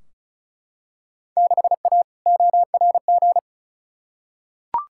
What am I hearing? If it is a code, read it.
Morse code, heard as a keyed tone, about 34 wpm, 700 Hz. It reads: 6A ORG
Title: Field Day Exchanges Encore - 35wpm